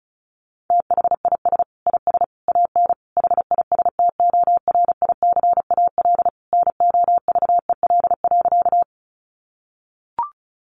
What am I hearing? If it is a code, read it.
Morse code, heard as a keyed tone, about 34 wpm, 700 Hz. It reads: THIS IS AN HISTORICAL NOVEL.
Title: Sentences from Top 1300 Words 35wpm